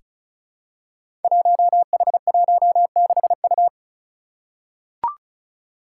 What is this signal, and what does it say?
Morse code, heard as a keyed tone, about 36 wpm, 700 Hz. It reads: 1H16U
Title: Five Challenging Letter-Number Combinations 35wpm